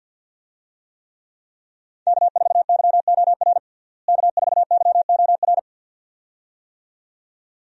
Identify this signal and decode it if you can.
Morse code, heard as a keyed tone, about 51 wpm, 700 Hz. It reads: K4YOR K4YOR
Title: USA Call Signs - 5 Characters - Repeated 2x - 50wpm